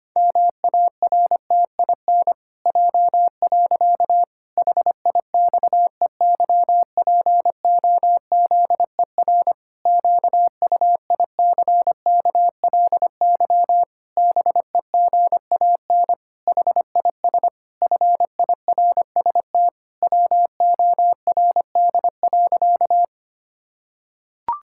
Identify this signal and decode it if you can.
Morse code, heard as a keyed tone, about 25 wpm, 700 Hz. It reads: MARTIN J. HIXEYPOZER QUICKLY BEGAN HIS FIRST WORD.
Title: Pangram Challenge v2 - 25wpm